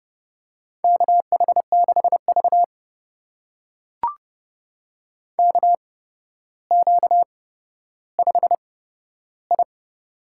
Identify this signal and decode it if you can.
Morse code, heard as a keyed tone, about 31 wpm, spaced as for 13 wpm, 700 Hz. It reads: KH6V KQ5I
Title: USA Call Signs - 8x Character Spacing - 30wpm